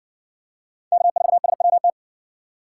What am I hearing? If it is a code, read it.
Morse code, heard as a keyed tone, about 60 wpm, 700 Hz. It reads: K4SWT